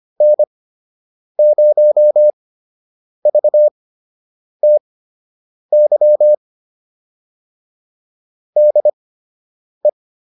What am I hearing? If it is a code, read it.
Morse code, heard as a keyed tone, about 25 wpm, spaced as for 8 wpm, 600 Hz. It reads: N0VTY DE